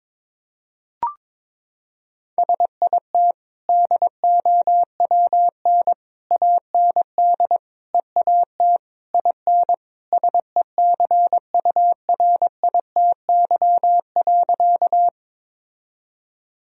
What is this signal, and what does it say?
Morse code, heard as a keyed tone, about 22 wpm, 700 Hz. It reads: SIT DOWN AND EAT IN SECURITY.
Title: Sentences from Top 1000 Words - Review Part 4 - 22wpm